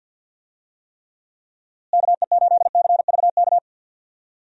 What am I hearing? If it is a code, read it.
Morse code, heard as a keyed tone, about 51 wpm, 700 Hz. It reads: KE8CVK
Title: USA Call Signs - 6 Characters - Repeated 2x - 50wpm